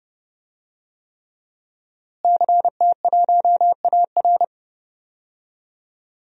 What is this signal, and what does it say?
Morse code, heard as a keyed tone, about 30 wpm, 700 Hz. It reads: CT1AR
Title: DXCC-100 Call Signs - 5 Characters - Repeated 3x - 30wpm